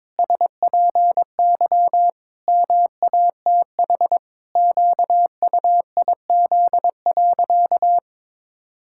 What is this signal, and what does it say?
Morse code, heard as a keyed tone, about 22 wpm, 700 Hz. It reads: SPY MATH QUIZ.
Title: Pangram Challenge v2 - 22wpm